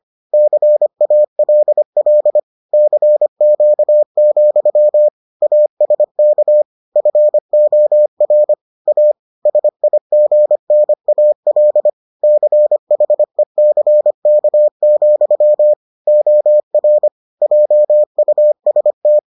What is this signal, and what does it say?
Morse code, heard as a keyed tone, about 25 wpm, 600 Hz. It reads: CALL CQ, ASK FOR A SIGNAL CHECK, OR JUST